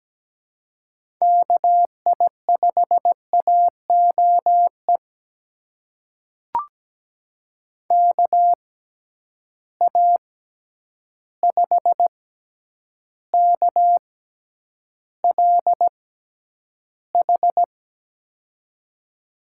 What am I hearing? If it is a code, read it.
Morse code, heard as a keyed tone, about 17 wpm, spaced as for 9 wpm, 700 Hz. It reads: KI5AOE KA5KLH
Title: USA Call Signs - 6 Characters - 6x Character Spacing - 17wpm